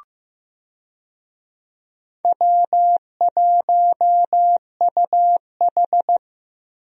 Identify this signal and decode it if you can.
Morse code, heard as a keyed tone, about 15 wpm, 700 Hz. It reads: W1UH
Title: USA Call Signs - 4 Characters - Repeated 2x - 15wpm